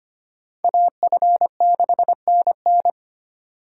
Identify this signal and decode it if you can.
Morse code, heard as a keyed tone, about 32 wpm, 700 Hz. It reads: AF6NN